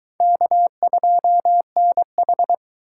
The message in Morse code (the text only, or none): K2NH